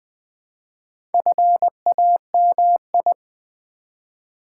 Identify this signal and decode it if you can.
Morse code, heard as a keyed tone, about 20 wpm, 700 Hz. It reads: FAMI
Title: Build A Word - 2 Letters At A Time - 8-Letter Words - 20wpm